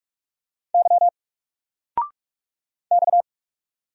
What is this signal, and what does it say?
Morse code, heard as a keyed tone, about 46 wpm, 700 Hz. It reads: Y X
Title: Single Letter-Number 45wpm